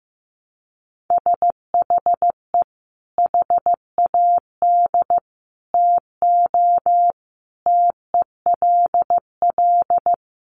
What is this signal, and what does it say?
Morse code, heard as a keyed tone, about 15 wpm, 700 Hz. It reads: SHE HAD TO TELL
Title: Sentences from Top 200 Words 15wpm